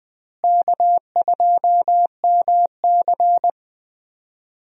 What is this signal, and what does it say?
Morse code, heard as a keyed tone, about 20 wpm, 700 Hz. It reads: K2MC
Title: USA Call Signs - 4 Characters - Repeated 2x - 20wpm